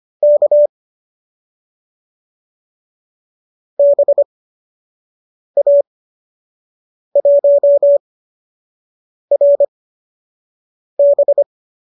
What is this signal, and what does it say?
Morse code, heard as a keyed tone, about 25 wpm, spaced as for 6 wpm, 570 Hz. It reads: K BA1RB
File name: qso002 06